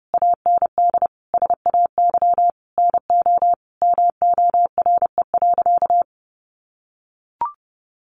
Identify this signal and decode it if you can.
Morse code, heard as a keyed tone, about 28 wpm, 700 Hz. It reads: AND SAY NO MORE.
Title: Sentences from Top 1400 Words 30wpm